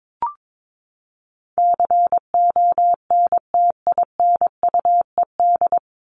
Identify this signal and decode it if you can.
Morse code, heard as a keyed tone, about 22 wpm, 700 Hz. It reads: CONTINUED
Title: ED Suffix - Top 100 Words with Base Words 22wpm